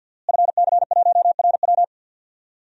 Morse code, heard as a keyed tone, about 46 wpm, 700 Hz. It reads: FC1RW
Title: DXCC-100 Call Signs - 5 Characters - Repeated 3x - 50wpm